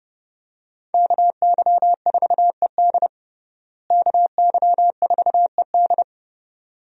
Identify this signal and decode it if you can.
Morse code, heard as a keyed tone, about 30 wpm, 700 Hz. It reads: KY4ED KY4ED
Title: USA Call Signs - 5 Characters - Repeated 2x - 30wpm